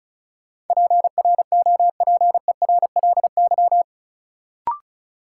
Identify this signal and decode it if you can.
Morse code, heard as a keyed tone, about 34 wpm, 700 Hz. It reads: PROPERLY
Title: Sentences from Top 1100 Words - New Word Drill - 35wpm